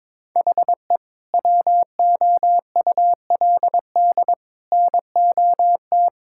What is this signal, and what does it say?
Morse code, heard as a keyed tone, about 22 wpm, 700 Hz. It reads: HE WOULD NOT
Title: Sentences from Top 1000 Words - Review Part 5 - 22wpm